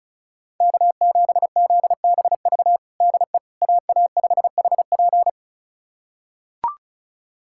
Morse code, heard as a keyed tone, about 36 wpm, 700 Hz. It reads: K7ZBV DE AA5HP
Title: USA Callsign DE Callsign - 5 Characters - 35wpm